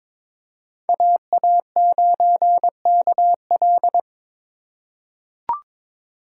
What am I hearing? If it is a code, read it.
Morse code, heard as a keyed tone, about 23 wpm, 700 Hz. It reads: AA9KL
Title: USA Call Signs - 5 Characters - 4x Character Spacing - 22wpm